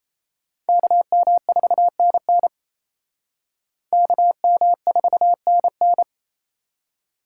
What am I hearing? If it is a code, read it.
Morse code, heard as a keyed tone, about 37 wpm, 700 Hz. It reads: KM4NN KM4NN